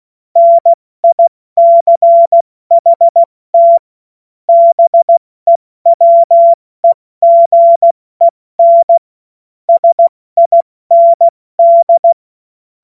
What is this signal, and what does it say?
Morse code, heard as a keyed tone, about 16 wpm, 680 Hz. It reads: NICHT BEWEGEN SIND
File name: Morsekurs / Witze / Witz 20 Februar 2024 70 F80